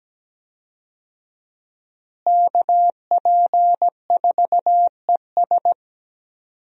Morse code, heard as a keyed tone, about 17 wpm, 700 Hz. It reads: KP4ES
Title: USA Call Signs - 5 Characters - Repeated 3x - 17wpm